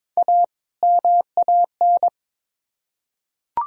Morse code, heard as a keyed tone, about 22 wpm, 700 Hz. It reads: A MAN
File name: Build A Phrase - 2-Word Phrases - 22wpm